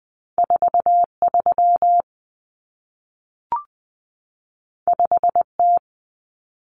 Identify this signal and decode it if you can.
Morse code, heard as a keyed tone, about 20 wpm, 700 Hz. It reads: 43 5T
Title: Common Cut Numbers - 2-Digit 20wpm